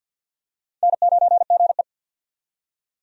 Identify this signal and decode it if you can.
Morse code, heard as a keyed tone, about 53 wpm, 700 Hz. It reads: N9GE
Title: USA Call Signs - 4 Characters - 50wpm